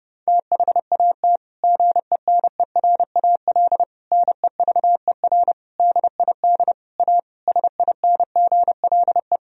Morse code, heard as a keyed tone, about 30 wpm, 700 Hz. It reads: THAT GENERAL NEVER DID A SINGLE
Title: Sentences from Top 1000 Words - Review Part 4 - 30wpm